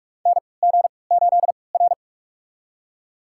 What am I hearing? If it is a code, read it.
Morse code, heard as a keyed tone, about 46 wpm, spaced as for 25 wpm, 700 Hz. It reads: NG8R
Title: USA Call Signs - 4 Characters - 3x Character Spacing - 45wpm